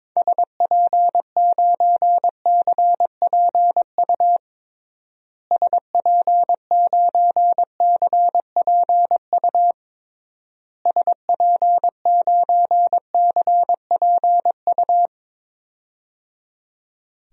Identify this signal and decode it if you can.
Morse code, heard as a keyed tone, about 22 wpm, 700 Hz. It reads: SP9CPU SP9CPU SP9CPU
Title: DXCC-100 Call Signs - 6 Characters - Repeated 3x - 22wpm